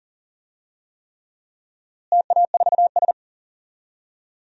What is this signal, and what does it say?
Morse code, heard as a keyed tone, about 40 wpm, 700 Hz. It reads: TA4S
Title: DXCC-100 Call Signs - 4 Characters - Repeated 3x - 40wpm